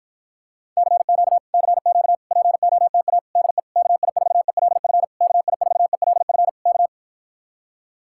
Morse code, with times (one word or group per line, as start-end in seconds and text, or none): 0.8-1.4 CQ
1.5-2.1 CQ
2.3-3.2 POTA
3.4-3.6 DE
3.8-5.0 KI4ELF
5.2-6.5 KI4ELF
6.7-6.9 K